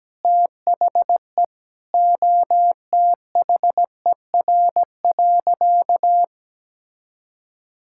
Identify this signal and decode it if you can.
Morse code, heard as a keyed tone, about 17 wpm, 700 Hz. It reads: THE OTHER.
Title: Sentences from Top 1000 Words - Review Part 5 - 17wpm